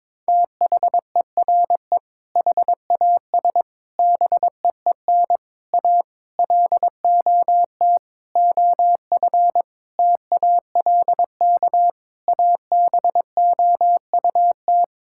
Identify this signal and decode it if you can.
Morse code, heard as a keyed tone, about 22 wpm, 700 Hz. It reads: THERE HAS BEEN A LOT OF TALK ABOUT
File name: practice / 22wpm / Sets of 8 Words - 22wpm